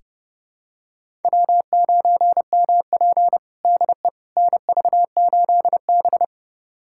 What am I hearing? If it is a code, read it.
Morse code, heard as a keyed tone, about 32 wpm, 700 Hz. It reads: W9MP DE NV8B
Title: USA Callsign DE Callsign - 4 Characters - 30wpm